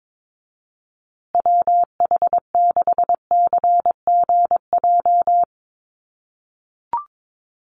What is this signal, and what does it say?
Morse code, heard as a keyed tone, about 22 wpm, 700 Hz. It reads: WH6CGJ
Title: USA Call Signs Repeated 3x 22wpm